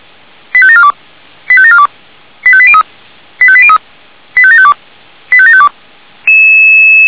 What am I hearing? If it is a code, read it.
Five-tone selective call (ZVEI), heard as quick runs of five tones, tones of about 70 ms. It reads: 85742 85742 85803 85803 85752 85752